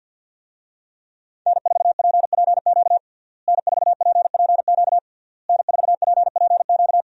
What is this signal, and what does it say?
Morse code, heard as a keyed tone, about 46 wpm, 700 Hz. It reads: N4PPQ N4PPQ N4PPQ
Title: USA Call Signs - 5 Characters - Repeated 3x - 50wpm